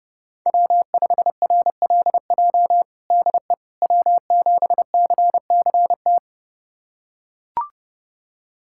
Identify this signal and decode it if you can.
Morse code, heard as a keyed tone, about 32 wpm, 700 Hz. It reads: W5RLJ DE W7CCT
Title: USA Callsign DE Callsign - 5 Characters - 30wpm